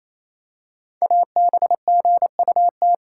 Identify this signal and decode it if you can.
Morse code, heard as a keyed tone, about 29 wpm, 700 Hz. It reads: ABGUT
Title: Code Groups - 5 Letters - 28wpm